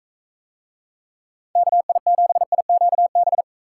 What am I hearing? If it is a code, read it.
Morse code, heard as a keyed tone, about 43 wpm, 700 Hz. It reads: KI7IQB